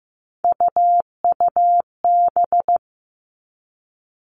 Morse code, heard as a keyed tone, about 15 wpm, 700 Hz. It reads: UUB
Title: Three Challenging Letter-Number Combinations 15wpm